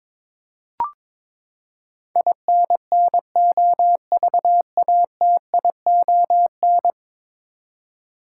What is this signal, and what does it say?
Morse code, heard as a keyed tone, about 22 wpm, 700 Hz. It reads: INNOVATION
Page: morse-code-ninja-practice-22wpm